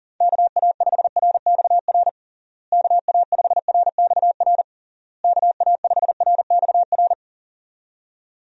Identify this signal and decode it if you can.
Morse code, heard as a keyed tone, about 37 wpm, 700 Hz. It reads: KA5RXR KA5RXR KA5RXR
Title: USA Call Signs - 6 Characters - Repeated 3x - 40wpm